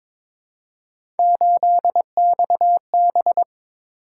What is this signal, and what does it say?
Morse code, heard as a keyed tone, about 23 wpm, 700 Hz. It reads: 8XB
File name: practice / 22wpm / Three Letter-Number Combinations - 22wpm